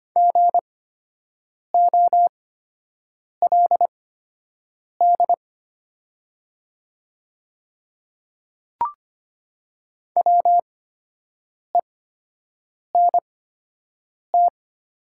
Morse code, heard as a keyed tone, about 25 wpm, spaced as for 7 wpm, 700 Hz. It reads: GOLD WENT
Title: 8x Character Spacing - Top 500 Words - 25wpm